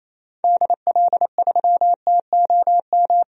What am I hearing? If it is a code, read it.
Morse code, heard as a keyed tone, about 28 wpm, 700 Hz. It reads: DL3TOM
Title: DXCC-100 Call Signs - 6 Characters - 28wpm